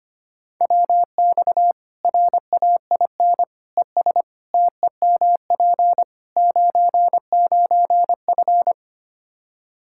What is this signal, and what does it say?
Morse code, heard as a keyed tone, about 25 wpm, 700 Hz. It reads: WX RAIN ES TEMP 99F
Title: Practice QSO - Exchange 2 - Info - 25wpm